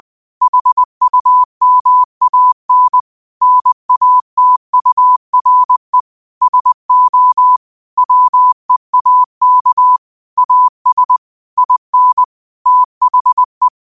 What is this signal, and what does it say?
Morse code, heard as a keyed tone, about 20 wpm, 1000 Hz. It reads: HUMAN NATURE SO WEAK AS IN THE
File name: qotdfiles / 20wpm